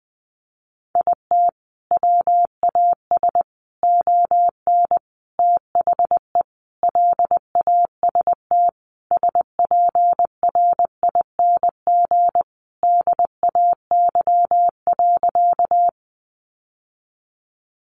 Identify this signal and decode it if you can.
Morse code, heard as a keyed tone, about 20 wpm, 700 Hz. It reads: IT WAS ON THE LAST SPRING DAY.
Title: Sentences from Top 1200 Words 20wpm